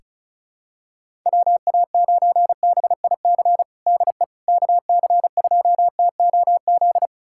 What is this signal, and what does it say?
Morse code, heard as a keyed tone, about 34 wpm, 700 Hz. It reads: WA9BIC DE KC2TOZ
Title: USA Callsign DE Callsign 35wpm